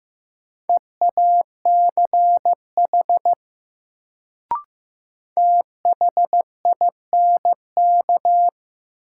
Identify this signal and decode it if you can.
Morse code, heard as a keyed tone, about 15 wpm, 700 Hz. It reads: EACH THINK
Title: Sentences from Top 200 Words - New Word Drill - 15wpm